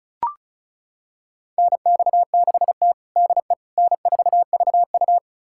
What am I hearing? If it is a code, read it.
Morse code, heard as a keyed tone, about 36 wpm, 700 Hz. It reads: NX6T DE N4VU
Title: USA Callsign DE Callsign - 4 Characters - 35wpm